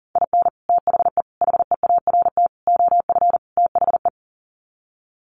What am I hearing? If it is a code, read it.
Morse code, heard as a keyed tone, about 37 wpm, 700 Hz. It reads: IN THE HEART OF THE
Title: Sets of 5 Words 40wpm